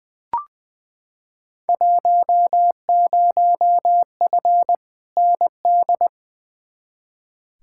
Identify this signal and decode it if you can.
Morse code, heard as a keyed tone, about 20 wpm, 700 Hz. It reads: 10F ND
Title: Field Day Exchanges Encore - 20wpm